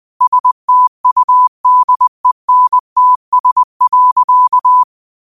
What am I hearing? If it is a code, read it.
Morse code, heard as a keyed tone, about 20 wpm, 1000 Hz. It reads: STUDENTS.
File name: qotdfiles / 20wpm